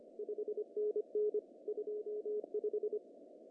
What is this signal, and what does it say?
Morse code, heard as a keyed tone, about 26 wpm, 410 Hz. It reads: 5NN 25
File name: CW / cqww-cw-k5tr